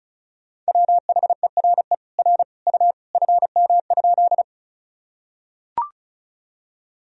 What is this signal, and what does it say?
Morse code, heard as a keyed tone, about 36 wpm, 700 Hz. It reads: WHERE R U FM?
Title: Sentences from Top 200 Words and Abbreviations 35wpm